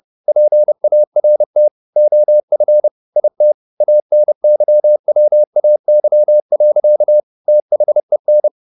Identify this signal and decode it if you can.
Morse code, heard as a keyed tone, about 30 wpm, 600 Hz. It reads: PART OF IT ANYWAY. THEN